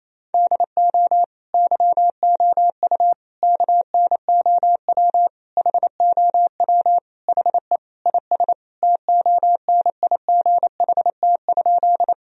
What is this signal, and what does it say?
Morse code, heard as a keyed tone, about 28 wpm, 700 Hz. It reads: DO YOU KNOW HOW HE IS TONIGHT?
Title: Sentences from Top 1000 Words - Review Part 3 - 28wpm